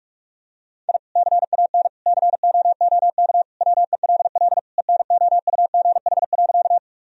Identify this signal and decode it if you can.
Morse code, heard as a keyed tone, about 46 wpm, 700 Hz. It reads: I CAN COOK WELL ENOUGH.
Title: Sentences from Top 1000 Words - Review Part 4 - 45wpm